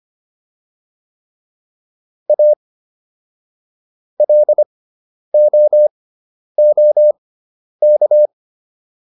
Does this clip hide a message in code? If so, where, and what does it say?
Yes: Morse code from 2.3 to 8.3 s, heard as a keyed tone, about 25 wpm, spaced as for 10 wpm, 600 Hz. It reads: A LOOK